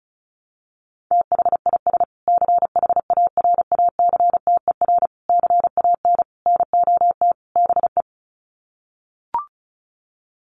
Morse code, heard as a keyed tone, about 34 wpm, 700 Hz. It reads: THIS CHARACTER CAN NOT BE
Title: Sets of 5 Words Encore2 35wpm